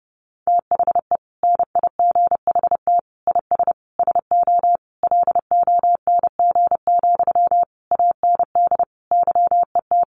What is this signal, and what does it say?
Morse code, heard as a keyed tone, about 30 wpm, 700 Hz. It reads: THE NIGHT IS SO LONG, AND YET
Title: Sentences from Top 500 Words 30wpm